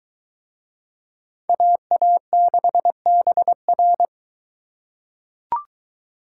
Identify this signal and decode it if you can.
Morse code, heard as a keyed tone, about 21 wpm, 700 Hz. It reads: AA6BR